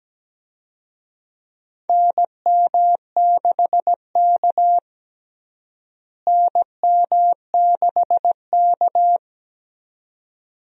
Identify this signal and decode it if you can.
Morse code, heard as a keyed tone, about 17 wpm, 700 Hz. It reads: NM6K NM6K
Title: USA Call Signs - 4 Characters - Repeated 2x - 17wpm